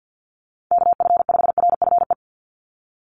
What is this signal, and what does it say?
Morse code, heard as a keyed tone, about 50 wpm, 700 Hz. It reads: KF5RFE
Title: USA Call Signs 50wpm